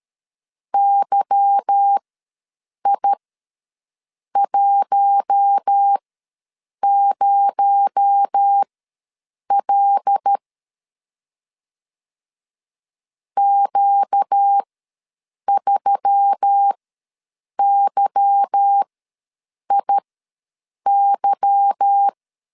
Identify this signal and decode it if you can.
Morse code, heard as a keyed tone, about 13 wpm, spaced as for 7 wpm, 800 Hz. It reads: YI10L Q3YIY
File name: LEZIONI / 8 YJ45 / 8 Y ws 12wpm 3409